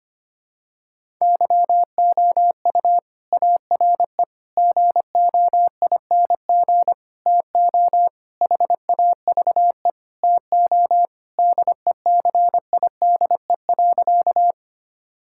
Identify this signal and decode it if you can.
Morse code, heard as a keyed tone, about 25 wpm, 700 Hz. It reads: YOU ARE GOING TO HAVE TO DECIDE.
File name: practice / 25wpm / Sentences with 4-Word Phrases - 25wpm